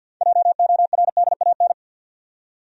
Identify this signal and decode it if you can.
Morse code, heard as a keyed tone, about 52 wpm, 700 Hz. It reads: JORDAN
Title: USA Common Names - 50wpm